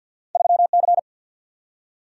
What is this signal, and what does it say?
Morse code, heard as a keyed tone, about 54 wpm, 700 Hz. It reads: 3C
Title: Single-Letter Single-Number 50wpm